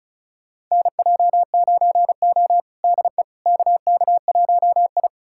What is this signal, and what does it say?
Morse code, heard as a keyed tone, about 36 wpm, 700 Hz. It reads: NJ9O DE KK1I
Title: USA Callsign DE Callsign - 4 Characters - 35wpm